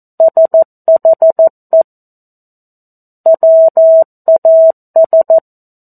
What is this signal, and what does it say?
Morse code, heard as a keyed tone, about 14 wpm, 650 Hz. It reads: SHE WAS
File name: 02-Chapter-02 14 WPM